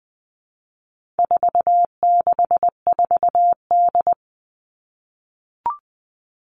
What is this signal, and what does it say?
Morse code, heard as a keyed tone, about 20 wpm, 700 Hz. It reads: 464D
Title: All Cut Numbers - 4-Digit 20wpm